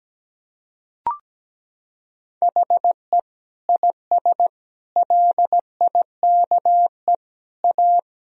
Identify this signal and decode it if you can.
Morse code, heard as a keyed tone, about 17 wpm, 700 Hz. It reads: HE IS LIKE A
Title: Sentences from Top 1000 Words - Review Part 1 - 17wpm